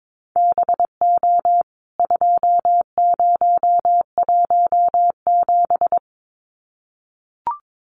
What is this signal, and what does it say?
Morse code, heard as a keyed tone, about 22 wpm, 700 Hz. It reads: BO 2017
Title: CWT Exchange 22wpm